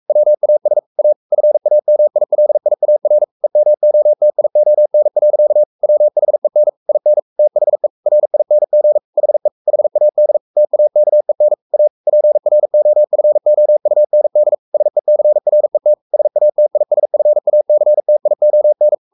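Morse code, heard as a keyed tone, about 44 wpm, 600 Hz. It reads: WAS A FAMILIAR EMOTION. WHEN IN THE RING HE HAD TAKEN A PROFOUND SECRET SATISFACTION
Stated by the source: Morse code at 43 wpm sending Ken Follett's novel Never in English, from CW med Gnister.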